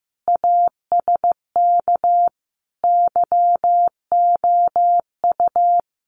Morse code, heard as a keyed tone, about 15 wpm, 700 Hz. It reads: ASK YOU
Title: Sets of 5 Words 15wpm